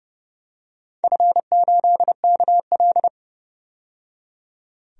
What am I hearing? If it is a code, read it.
Morse code, heard as a keyed tone, about 30 wpm, 700 Hz. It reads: F8KL
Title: DXCC-100 Call Signs - 4 Characters - 30wpm